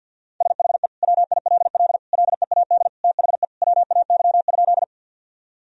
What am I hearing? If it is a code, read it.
Morse code, heard as a keyed tone, about 50 wpm, 700 Hz. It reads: SHE WILL LEAD THE WAY?
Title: Sentences from Top 1000 Words - Review Part 2 - 50wpm